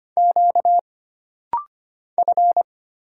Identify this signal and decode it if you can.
Morse code, heard as a keyed tone, about 25 wpm, 700 Hz. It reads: Q F
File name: Single Character (Letter:Number:Punctuation:Prosign) - Rapid-Fire - 25wpm